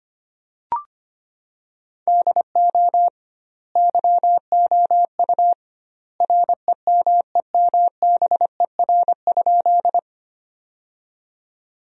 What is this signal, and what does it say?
Morse code, heard as a keyed tone, about 25 wpm, 700 Hz. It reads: DO YOU REMEMBER?
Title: 2x Word-Spacing - Sentences from Top 400 Words - 25wpm